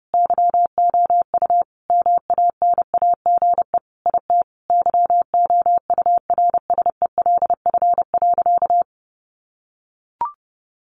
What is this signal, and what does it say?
Morse code, heard as a keyed tone, about 30 wpm, 700 Hz. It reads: YOU MANAGE IT YOURSELF.
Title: Sentences from Top 1400 Words 30wpm